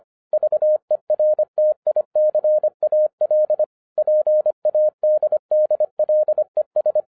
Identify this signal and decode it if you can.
Morse code, heard as a keyed tone, about 25 wpm, 610 Hz. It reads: VERTICAL PADDLES